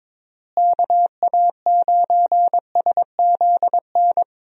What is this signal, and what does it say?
Morse code, heard as a keyed tone, about 22 wpm, 700 Hz. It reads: KA9SZN
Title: USA Call Signs - 6 Characters - Repeated 2x - 22wpm